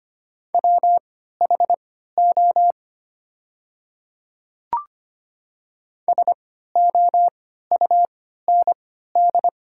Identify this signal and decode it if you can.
Morse code, heard as a keyed tone, about 25 wpm, spaced as for 14 wpm, 700 Hz. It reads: WHO SOUND
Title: 3x Character Spacing - Top 300 Words - 25wpm